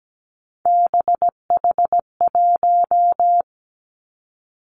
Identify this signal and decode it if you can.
Morse code, heard as a keyed tone, about 17 wpm, 700 Hz. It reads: BH1
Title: Three Challenging Letter-Number Combinations 17wpm